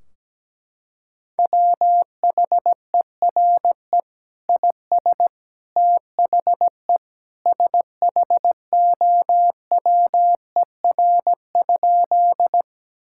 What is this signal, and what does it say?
Morse code, heard as a keyed tone, about 17 wpm, 700 Hz. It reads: WHERE IS THE SHOWER?
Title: Sentences from Top 1600 Words 17wpm